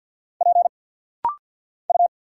Morse code, heard as a keyed tone, about 46 wpm, 690 Hz. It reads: P U
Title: Single Letters - Rapid-Fire 50wpm